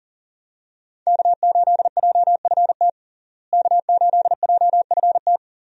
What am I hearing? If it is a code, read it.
Morse code, heard as a keyed tone, about 40 wpm, 700 Hz. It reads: K8JFT K8JFT
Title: USA Call Signs - 5 Characters - Repeated 2x - 40wpm